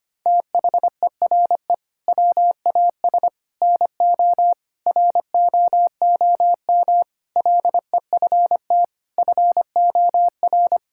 Morse code, heard as a keyed tone, about 25 wpm, 700 Hz. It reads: THERE WAS NO ROOM LEFT FOR
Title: Sentences from Top 1600 Words 25wpm